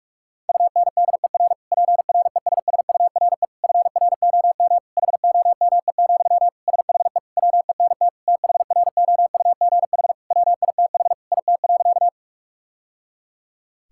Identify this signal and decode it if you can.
Morse code, heard as a keyed tone, about 41 wpm, 700 Hz. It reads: UNDER PRESSURE FROM HOME, SHE WENT THROUGH WITH IT.
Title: Sentences from Top 1000 Words - Review Part 6 - 45wpm